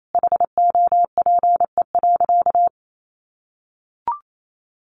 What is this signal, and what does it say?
Morse code, heard as a keyed tone, about 28 wpm, 700 Hz. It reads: HOPE.
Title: Sentences from Top 1000 Words 28wpm